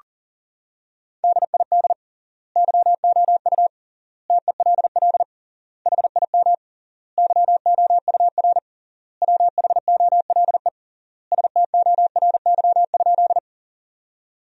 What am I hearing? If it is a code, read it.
Morse code, heard as a keyed tone, about 40 wpm, 700 Hz. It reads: DID YOU TELL HIM YOUR WHOLE STORY?
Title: 3x Word-Spacing - Sentences from Top 400 Words - 40wpm